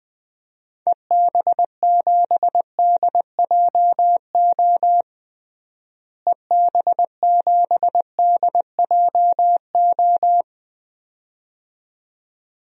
Morse code, heard as a keyed tone, about 20 wpm, 700 Hz. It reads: EB7DJO EB7DJO
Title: DXCC-100 Call Signs - 6 Characters - Repeated 2x - 20wpm